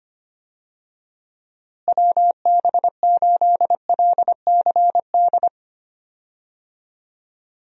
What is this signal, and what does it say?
Morse code, heard as a keyed tone, about 25 wpm, 700 Hz. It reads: WB8LCD
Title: USA Call Signs - 6 Characters - Repeated 2x - 25wpm